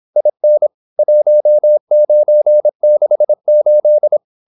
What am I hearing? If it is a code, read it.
Morse code, heard as a keyed tone, about 26 wpm, 600 Hz. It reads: IN 1968
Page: Never 26wpm | CW med Gnister